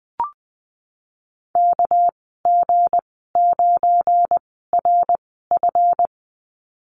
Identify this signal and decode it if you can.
Morse code, heard as a keyed tone, about 20 wpm, spaced as for 15 wpm, 700 Hz. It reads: KG9RF
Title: 231 Lesson 37g - Letter x - Call Signs 20wpm